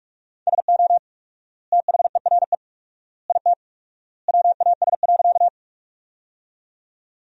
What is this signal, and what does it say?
Morse code, heard as a keyed tone, about 47 wpm, 700 Hz. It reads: SO THERE IT WAS.
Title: 4x Word-Spacing - Sentences from Top 100 Words - 45wpm